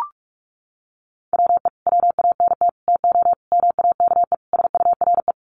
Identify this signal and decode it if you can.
Morse code, heard as a keyed tone, about 44 wpm, 700 Hz. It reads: WE WANT TO MAKE SURE
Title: Sets of 5 Words Encore2 45wpm